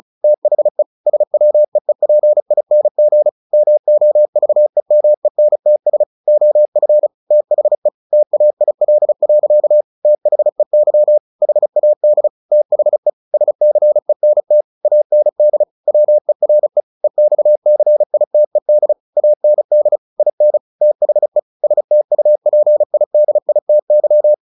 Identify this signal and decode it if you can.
Morse code, heard as a keyed tone, about 34 wpm, 600 Hz. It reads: THE SWEEPING MOVEMENTS OF THE TAIL. THEY HAD THE SCENT AND WERE EXCITED AND IN THE STUPIDITY
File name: the old man and the sea 2 35wpm